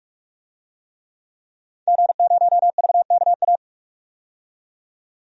Morse code, heard as a keyed tone, about 44 wpm, 700 Hz. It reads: G0VKA
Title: DXCC-100 Call Signs - 5 Characters - 45wpm